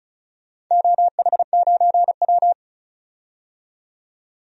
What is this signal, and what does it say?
Morse code, heard as a keyed tone, about 36 wpm, 700 Hz. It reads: OH9W
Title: DXCC-100 Call Signs - 4 Characters - 35wpm